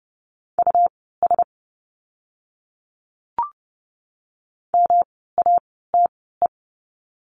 Morse code, heard as a keyed tone, about 30 wpm, spaced as for 17 wpm, 700 Hz. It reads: US MATE
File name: course / 30:10wpm / 108 Lesson 18f - Letter m - Words 30wpm